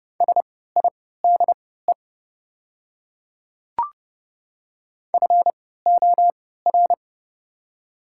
Morse code, heard as a keyed tone, about 30 wpm, spaced as for 17 wpm, 700 Hz. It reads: SIDE FOR
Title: 3x Character Spacing - Top 200 Words - 30wpm